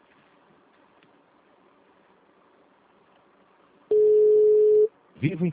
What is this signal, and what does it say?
Ringback tone (Europe, 425 Hz), with speech.